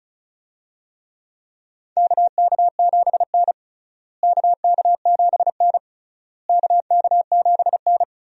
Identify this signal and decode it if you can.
Morse code, heard as a keyed tone, about 36 wpm, 700 Hz. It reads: KK7N KK7N KK7N
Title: USA Call Signs - 4 Characters - Repeated 3x - 35wpm